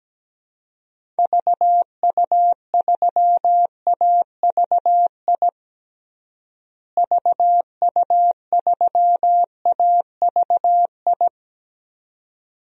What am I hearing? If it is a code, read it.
Morse code, heard as a keyed tone, about 17 wpm, 700 Hz. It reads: VU3AVI VU3AVI
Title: DXCC-100 Call Signs - 6 Characters - Repeated 2x - 17wpm